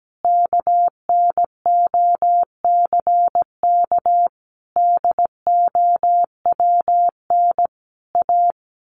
Morse code, heard as a keyed tone, about 17 wpm, 700 Hz. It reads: KNOCK DOWN A
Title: A Sound Of Thunder 17wpm